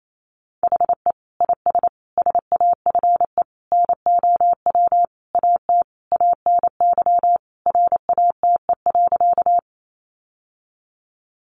Morse code, heard as a keyed tone, about 28 wpm, 700 Hz. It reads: HE IS SAFE NOW AT ANY RATE.
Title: Sentences from Top 900 Words 28wpm